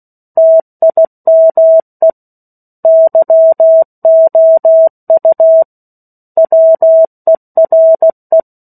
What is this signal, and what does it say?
Morse code, heard as a keyed tone, about 16 wpm, 650 Hz. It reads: TIME YOU WERE
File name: 03-Chapter-03 16 WPM